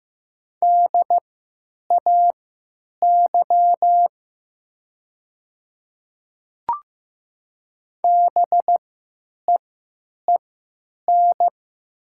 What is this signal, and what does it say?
Morse code, heard as a keyed tone, about 15 wpm, spaced as for 9 wpm, 700 Hz. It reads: DAY BEEN
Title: 3x Character Spacing - Top 200 Words - 15wpm